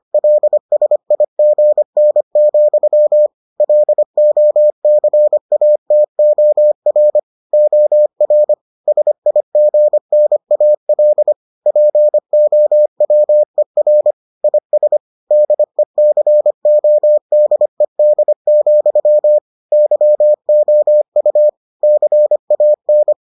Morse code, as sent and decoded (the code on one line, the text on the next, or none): .-.. ... .. --. -. --..-- / .-.. --- -.-. .- - --- .-. / --- .-. / ... .. --. -. .- .-.. / .--. --- .-- . .-. / .. ... / -.. -.-. --- -.. -.. --..-- / -.-- --- ..- / -.-. .- -.
LSIGN, LOCATOR OR SIGNAL POWER IS DECODED, YOU CAN